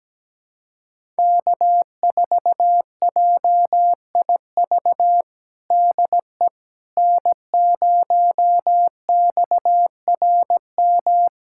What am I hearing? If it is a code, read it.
Morse code, heard as a keyed tone, about 17 wpm, 700 Hz. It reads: K4JIV DE N0XRM
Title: USA Callsign DE Callsign - 5 Characters - 17wpm